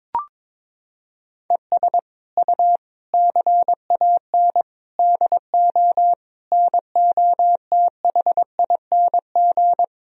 Morse code, heard as a keyed tone, about 22 wpm, 700 Hz. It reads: ES U CAN DO NOTHING
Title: Sentences from Top 500 Words and Abbreviations 22wpm